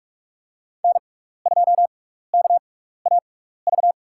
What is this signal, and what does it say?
Morse code, heard as a keyed tone, about 48 wpm, spaced as for 16 wpm, 700 Hz. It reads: N2KAV
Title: USA Call Signs - 5 Characters - 6x Character Spacing - 45wpm